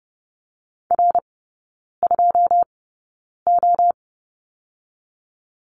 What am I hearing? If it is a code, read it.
Morse code, heard as a keyed tone, about 30 wpm, spaced as for 9 wpm, 700 Hz. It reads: R2O
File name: course / 30:10wpm / 087 Lesson 15e - Number 5 - Three Characters 30wpm